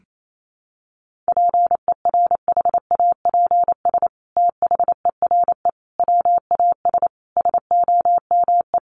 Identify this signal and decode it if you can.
Morse code, heard as a keyed tone, about 28 wpm, 700 Hz. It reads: PERHAPS THERE WAS SOME
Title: Sentences from Top 700 Words 28wpm